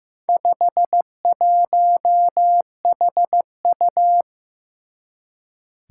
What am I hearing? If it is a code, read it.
Morse code, heard as a keyed tone, about 15 wpm, 700 Hz. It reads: 51HU
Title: Four Challenging Letter-Number Combinations 15wpm